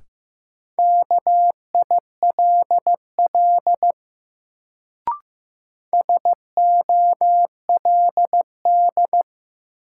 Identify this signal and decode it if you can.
Morse code, heard as a keyed tone, about 15 wpm, 700 Hz. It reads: KILL SOLD
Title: Sentences from Top 1400 Words - New Word Drill - 15wpm